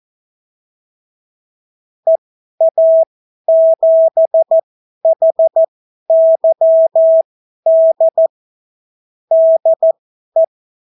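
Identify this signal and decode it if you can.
Morse code, heard as a keyed tone, about 14 wpm, spaced as for 11 wpm, 650 Hz. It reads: EA7HYD DE
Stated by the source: Example Spanish CW QSO from a Morse learning book, 14 wpm characters with 8 wpm spacing.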